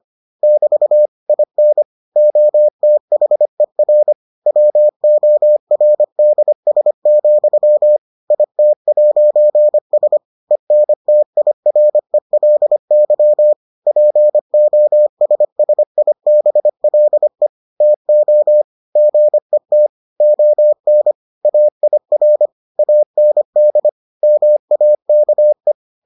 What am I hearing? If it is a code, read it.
Morse code, heard as a keyed tone, about 25 wpm, 600 Hz. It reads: <BT> IN OTHER WORDS, IT'S ENTIRELY POSSIBLE TO GET ON AIR AND MAKE